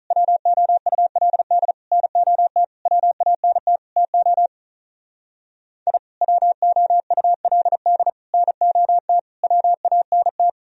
Morse code, heard as a keyed tone, about 43 wpm, 700 Hz. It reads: WOULD NOT WANT TO I WOULD NOT WANT
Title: Speed-Racing - Sets of 5 Words 35wpm